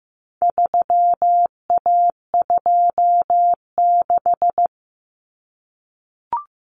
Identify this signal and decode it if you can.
Morse code, heard as a keyed tone, about 15 wpm, 700 Hz. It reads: 3A26
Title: Common Cut Numbers - 4-Digit 15wpm